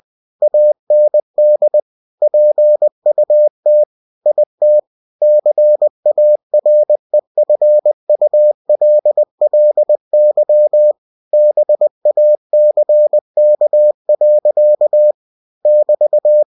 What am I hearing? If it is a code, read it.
Morse code, heard as a keyed tone, about 20 wpm, 600 Hz. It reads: AND PUT IT CAREFULLY BACK. <BT>
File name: cw-books / winnie-the-pooh / winnie-20wpm-page-0058